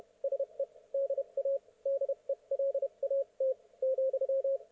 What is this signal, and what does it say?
Morse code, heard as a keyed tone, about 31 wpm, spaced as for 26 wpm, 530 Hz. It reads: SE DA DELAT ,
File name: Archiv / CUC107 / ok1fhi cul107